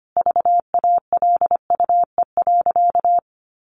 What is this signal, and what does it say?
Morse code, heard as a keyed tone, about 25 wpm, 700 Hz. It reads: VALUE.
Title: Sentences from Top 1300 Words 25wpm